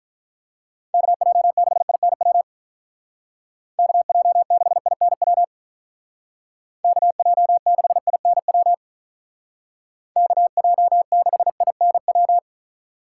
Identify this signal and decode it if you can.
Morse code, heard as a keyed tone, about 44 wpm, 700 Hz. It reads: KJ6INW KJ6INW KJ6INW KJ6INW
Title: Speed Racing - USA 2x3 Call Signs 35wpm